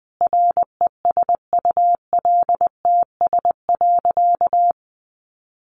Morse code, heard as a keyed tone, about 20 wpm, 700 Hz. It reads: RESULTS.
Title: A Sound Of Thunder 20wpm